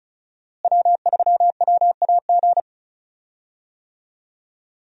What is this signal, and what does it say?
Morse code, heard as a keyed tone, about 34 wpm, 700 Hz. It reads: W3WAG
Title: USA Call Signs - 5 Characters - Repeated 2x - 35wpm